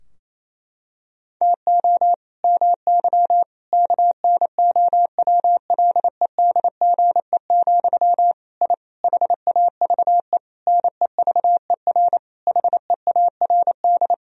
Morse code, heard as a keyed tone, about 28 wpm, 700 Hz. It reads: TO MY KNOWLEDGE, I HAVE NEVER HEARD
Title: Sentences from Top 1000 Words - Review Part 3 - 28wpm